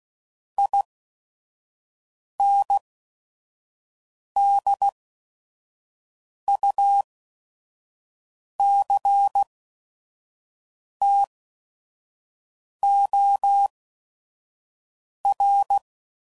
Morse code, heard as a keyed tone, about 16 wpm, spaced as for 5 wpm, 800 Hz. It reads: INDUCTOR
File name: CW Code Groups @5WPM / Technical Subjects / XL